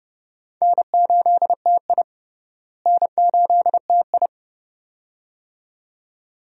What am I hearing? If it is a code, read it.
Morse code, heard as a keyed tone, about 30 wpm, 700 Hz. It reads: N8TI N8TI
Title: USA Call Signs - 4 Characters - Repeated 2x - 30wpm